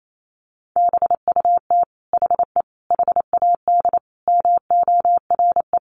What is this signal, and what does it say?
Morse code, heard as a keyed tone, about 28 wpm, 700 Hz. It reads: BUT HE HAD MORE
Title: Sentences from Top 1500 Words 28wpm